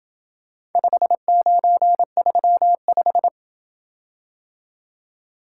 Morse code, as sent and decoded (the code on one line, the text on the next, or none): ..... ----. ...-- .....
5935